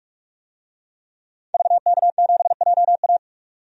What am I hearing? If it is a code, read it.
Morse code, heard as a keyed tone, about 44 wpm, 700 Hz. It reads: VK7JA
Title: DXCC-100 Call Signs - 5 Characters - 45wpm